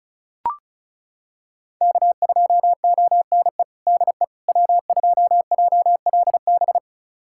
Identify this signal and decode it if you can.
Morse code, heard as a keyed tone, about 36 wpm, 700 Hz. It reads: K2ONE DE W2JLB
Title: USA Callsign DE Callsign - 5 Characters - 35wpm